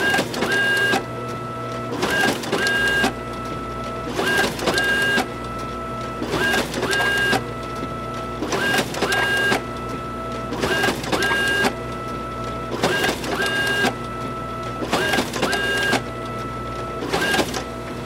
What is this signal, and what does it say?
There is no code.